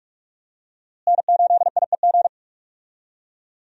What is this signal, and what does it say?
Morse code, heard as a keyed tone, about 46 wpm, 700 Hz. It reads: N8IEG
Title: USA Call Signs - 5 Characters - Repeated 2x - 45wpm